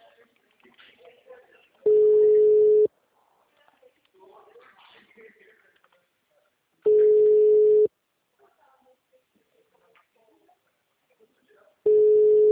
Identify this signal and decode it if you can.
Ringback tone (Europe, 425 Hz), with speech.
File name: out- -2013- - - .24